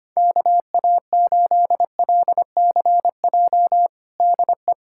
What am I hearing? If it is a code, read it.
Morse code, heard as a keyed tone, about 25 wpm, 700 Hz. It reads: KA8LCJ DE
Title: USA Callsign DE Callsign - 6 Characters - 25wpm